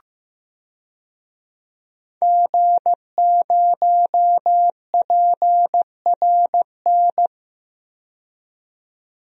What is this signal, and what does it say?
Morse code, heard as a keyed tone, about 15 wpm, 700 Hz. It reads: G0PRN
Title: DXCC-100 Call Signs - 5 Characters - 15wpm